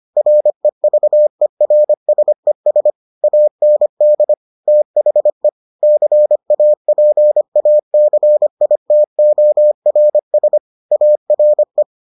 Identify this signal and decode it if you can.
Morse code, heard as a keyed tone, about 25 wpm, 600 Hz. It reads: REVERSES AND THE CAPACITORS ARE